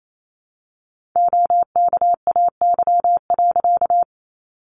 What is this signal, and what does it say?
Morse code, heard as a keyed tone, about 28 wpm, 700 Hz. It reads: OKAY.
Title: A Sound Of Thunder 28wpm